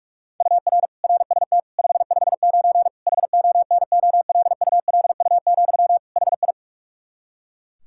Morse code, heard as a keyed tone, about 46 wpm, 700 Hz. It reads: UR RST 559 HONOLULU, HI
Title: Practice QSO - Exchange 1 - Beginning - 45wpm